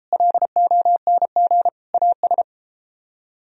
Morse code, heard as a keyed tone, about 31 wpm, 700 Hz. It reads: LONG AS